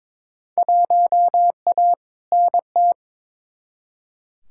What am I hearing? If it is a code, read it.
Morse code, heard as a keyed tone, about 21 wpm, 700 Hz. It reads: 1A NT
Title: Field Day Exchanges Encore - 22wpm